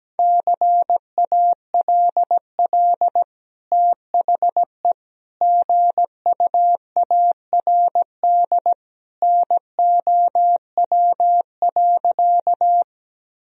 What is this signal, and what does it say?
Morse code, heard as a keyed tone, about 17 wpm, 700 Hz. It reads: CALL THE GUARD NOW.
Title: Sentences from Top 1000 Words - Review Part 4 - 17wpm